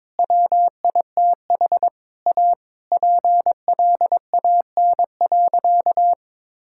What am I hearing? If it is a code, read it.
Morse code, heard as a keyed tone, about 21 wpm, 700 Hz. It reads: WITH A PLAN.
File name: practice / 22wpm / Sentences with 4-Word Phrases - 22wpm